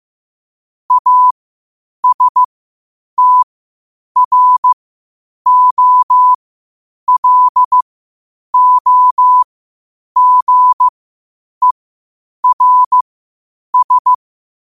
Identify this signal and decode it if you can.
Morse code, heard as a keyed tone, about 15 wpm, spaced as for 8 wpm, 1000 Hz. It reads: ASTROLOGERS
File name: qotdfiles / 8wpm